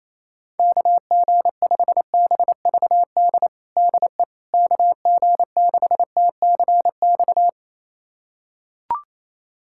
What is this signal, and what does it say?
Morse code, heard as a keyed tone, about 28 wpm, 700 Hz. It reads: KG5BVD DE KG6TCX
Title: USA Callsign DE Callsign 28wpm